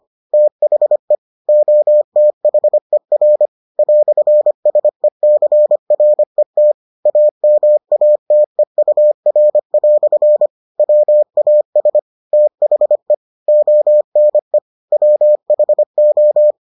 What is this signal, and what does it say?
Morse code, heard as a keyed tone, about 25 wpm, 600 Hz. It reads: THE OTHER "SECRET AMATEUR" WAS THE ONE WHO